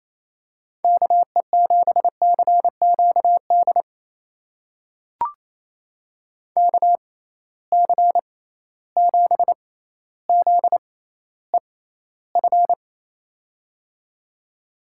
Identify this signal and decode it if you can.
Morse code, heard as a keyed tone, about 28 wpm, spaced as for 14 wpm, 700 Hz. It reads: KE7CQD KC7ZEF
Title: USA Call Signs - 6 Characters - 6x Character Spacing - 28wpm